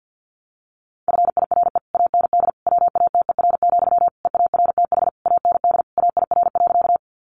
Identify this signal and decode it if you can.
Morse code, heard as a keyed tone, about 50 wpm, 700 Hz. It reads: FIRE AND WATER, EARTH AND AIR.
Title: Sentences from Top 400 Words 50wpm